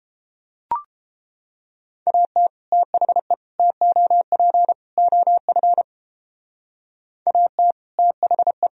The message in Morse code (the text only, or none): AT THE TOP OF AT THE